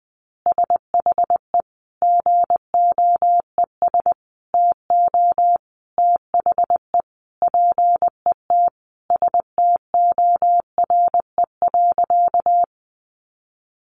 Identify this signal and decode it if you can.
Morse code, heard as a keyed tone, about 20 wpm, 700 Hz. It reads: SHE GOES TO THE PET STORE.
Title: Easy Phrases 20wpm